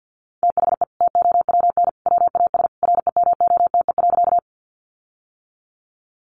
Morse code, heard as a keyed tone, about 51 wpm, 700 Hz. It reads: THE TOWN WAS REMOTE.
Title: Sentences from Top 1300 Words 50wpm